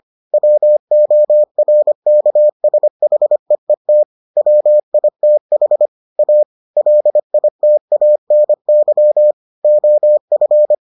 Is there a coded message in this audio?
Yes: Morse code from 0.3 to 10.7 s, heard as a keyed tone, about 25 wpm, 600 Hz. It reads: WORKSHEET WITH A LITANY OF